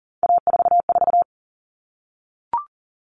Morse code, heard as a keyed tone, about 40 wpm, 700 Hz. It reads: A44
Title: Common Cut Numbers - 3-Digit 40wpm